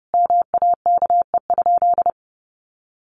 Morse code, heard as a keyed tone, about 30 wpm, 700 Hz. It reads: MAKE?
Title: Sentences from Top 1400 Words 30wpm